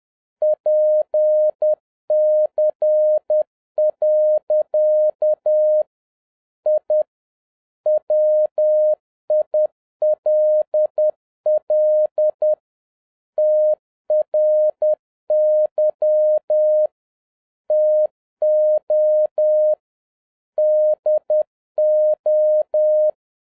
Morse code, heard as a keyed tone, about 10 wpm, 610 Hz. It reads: PC. I WILL TRY TO DO